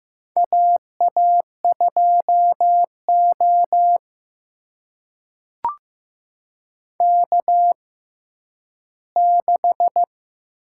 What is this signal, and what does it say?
Morse code, heard as a keyed tone, about 15 wpm, spaced as for 10 wpm, 700 Hz. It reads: AA2O K6
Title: USA Call Signs - 4 Characters - 6x Character Spacing - 15wpm